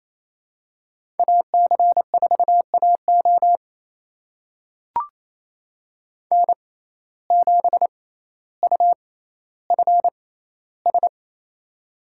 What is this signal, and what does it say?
Morse code, heard as a keyed tone, about 28 wpm, spaced as for 14 wpm, 700 Hz. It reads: AC4AO N7UFS
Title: USA Call Signs - 5 Characters - 6x Character Spacing - 28wpm